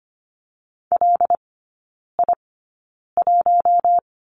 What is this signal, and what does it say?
Morse code, heard as a keyed tone, about 25 wpm, spaced as for 9 wpm, 700 Hz. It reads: LI1
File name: Lesson 17e - Letter u - Three Characters 25wpm